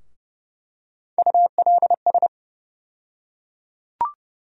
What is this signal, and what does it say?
Morse code, heard as a keyed tone, about 30 wpm, 700 Hz. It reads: ULS
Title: USA Call Sign Postfixes - 3 Letters 30wpm